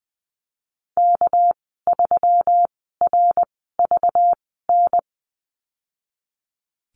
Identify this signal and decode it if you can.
Morse code, heard as a keyed tone, about 20 wpm, spaced as for 15 wpm, 700 Hz. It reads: K3RVN
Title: 213 Lesson 34g - Number 8 - Call Signs 20wpm